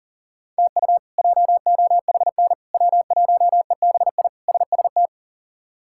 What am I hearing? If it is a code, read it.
Morse code, heard as a keyed tone, about 40 wpm, 700 Hz. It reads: TU JOHN W1EBI SST
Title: K1USN SST QSO Practice - 40wpm